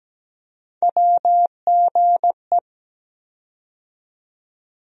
Morse code, heard as a keyed tone, about 17 wpm, 700 Hz. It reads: WGE
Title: USA Call Sign Postfixes - 3 Letters 17wpm